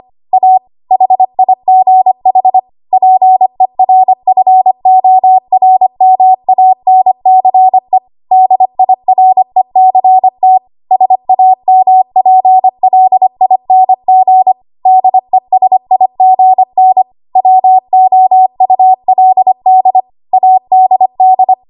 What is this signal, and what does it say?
Morse code, heard as a keyed tone, about 25 wpm, 750 Hz. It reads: A HIGH PERFORMANCE DIRECT SAMPLING DESIGN WOULD ADD
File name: Archive / 25 WPM / 25WPM